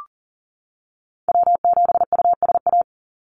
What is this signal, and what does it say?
Morse code, heard as a keyed tone, about 40 wpm, 700 Hz. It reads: W7USA
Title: NA SOTA Chaser Callsigns 40wpm